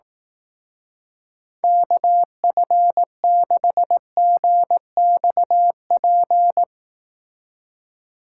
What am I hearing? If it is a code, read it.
Morse code, heard as a keyed tone, about 18 wpm, 700 Hz. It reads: KF6GXP
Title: Speed Racing - USA Call Signs 15wpm